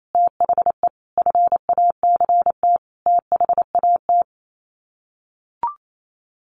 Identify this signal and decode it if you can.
Morse code, heard as a keyed tone, about 28 wpm, 700 Hz. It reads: THE FACT THAT
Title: Sets of 4 Words 28wpm